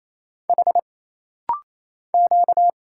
Morse code, heard as a keyed tone, about 27 wpm, 700 Hz. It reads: H Q
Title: Single Letters - Rapid-Fire 28wpm